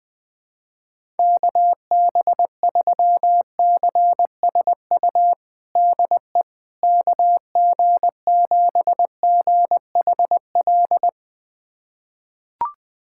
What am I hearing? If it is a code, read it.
Morse code, heard as a keyed tone, about 20 wpm, 700 Hz. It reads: KB3CSU DE KG7GHL
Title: USA Callsign DE Callsign - 6 Characters - 20wpm